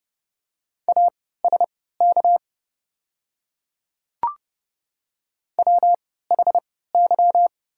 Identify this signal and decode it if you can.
Morse code, heard as a keyed tone, about 30 wpm, spaced as for 17 wpm, 700 Hz. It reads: ASK WHY
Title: 3x Character Spacing - Top 200 Words - 30wpm